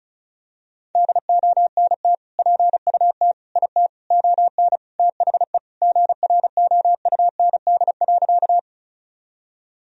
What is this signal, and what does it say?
Morse code, heard as a keyed tone, about 36 wpm, 700 Hz. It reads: DONT PUT IT ON THE GROUND.
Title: Sentences from Top 1000 Words - Review Part 2 - 35wpm